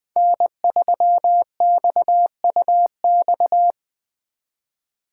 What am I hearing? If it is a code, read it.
Morse code, heard as a keyed tone, about 21 wpm, 700 Hz. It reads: N3XUX